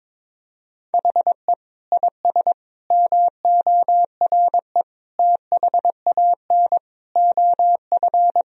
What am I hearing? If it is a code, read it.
Morse code, heard as a keyed tone, about 22 wpm, 700 Hz. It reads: HE IS MORE THAN OF
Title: Sentences from Top 1000 Words - Review Part 3 - 22wpm